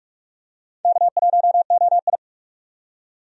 Morse code, heard as a keyed tone, about 42 wpm, 700 Hz. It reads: K1OI
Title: USA Call Signs - 4 Characters - 45wpm